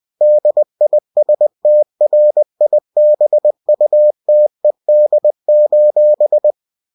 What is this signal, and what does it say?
Morse code, heard as a keyed tone, about 20 wpm, 600 Hz. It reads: DISTRIBUTED:
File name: winnie-20wpm